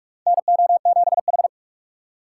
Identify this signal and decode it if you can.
Morse code, heard as a keyed tone, about 44 wpm, 700 Hz. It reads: NO7H
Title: USA Call Signs - 4 Characters - 4x Character Spacing - 45wpm